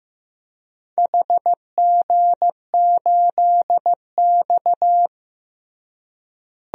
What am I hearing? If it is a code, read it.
Morse code, heard as a keyed tone, about 15 wpm, 700 Hz. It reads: HG8X
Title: DXCC-100 Call Signs - 4 Characters - Repeated 2x - 15wpm